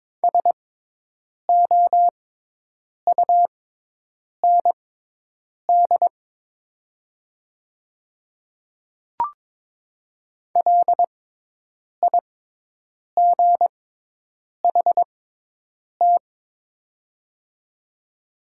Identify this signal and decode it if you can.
Morse code, heard as a keyed tone, about 22 wpm, spaced as for 8 wpm, 700 Hz. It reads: SOUND LIGHT
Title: 6x Character Spacing - Top 300 Words - 22wpm